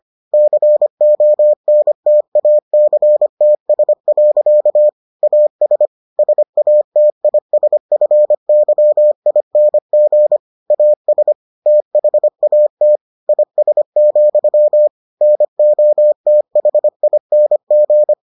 Morse code, heard as a keyed tone, about 25 wpm, 600 Hz. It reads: CONTACTS. AS SATISFYING AS THAT IS, NOTHING